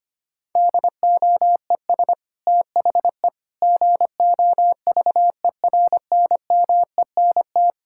Morse code, heard as a keyed tone, about 25 wpm, 700 Hz. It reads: DOES THE GOVERNMENT
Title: Sentences from Top 1600 Words 25wpm